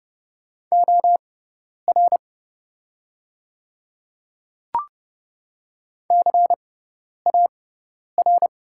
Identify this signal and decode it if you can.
Morse code, heard as a keyed tone, about 30 wpm, spaced as for 10 wpm, 700 Hz. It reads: OR CAR
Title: 6x Character Spacing - Top 300 Words - 30wpm